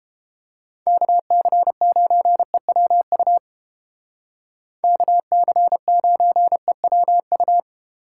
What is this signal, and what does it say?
Morse code, heard as a keyed tone, about 28 wpm, 700 Hz. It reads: KC9EWU KC9EWU